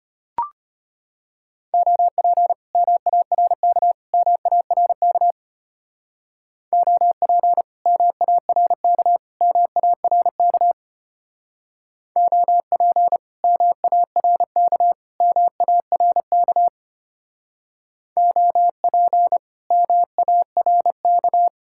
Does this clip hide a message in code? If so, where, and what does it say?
Yes: Morse code from 1.7 to 21.5 s, heard as a keyed tone, about 32 wpm, 700 Hz. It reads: OP MARK MARK OP MARK MARK OP MARK MARK OP MARK